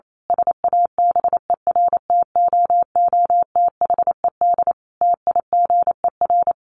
Morse code, heard as a keyed tone, about 28 wpm, 700 Hz. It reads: SABERTOOTHED TIGER
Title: A Sound Of Thunder 28wpm